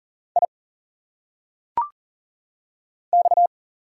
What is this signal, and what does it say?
Morse code, heard as a keyed tone, about 39 wpm, 700 Hz. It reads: I X
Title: Single Letters - Phonetic Pronunciation 40wpm